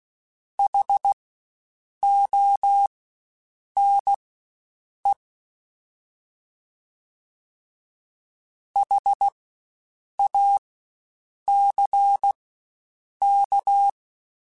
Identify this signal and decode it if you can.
Morse code, heard as a keyed tone, about 16 wpm, spaced as for 7 wpm, 800 Hz. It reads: HONE HACK
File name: Spaced Alphas / Spaced Alphas mp3s / Hs sp @15